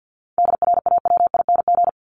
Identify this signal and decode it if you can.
Morse code, heard as a keyed tone, about 50 wpm, 700 Hz. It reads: DRAWING
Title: ING Suffix - Top 100 Words with Base Words 50wpm